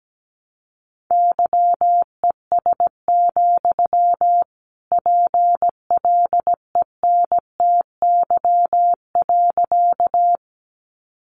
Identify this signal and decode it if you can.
Morse code, heard as a keyed tone, about 17 wpm, 700 Hz. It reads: YES, PLENTY.
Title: Sentences from Top 1300 Words 17wpm